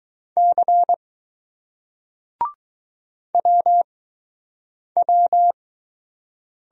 Morse code, heard as a keyed tone, about 20 wpm, spaced as for 7 wpm, 700 Hz. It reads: C WW